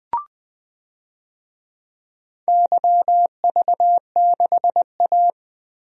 Morse code, heard as a keyed tone, about 20 wpm, 700 Hz. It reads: YV6A
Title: DXCC-100 Call Signs - 4 Characters - 20wpm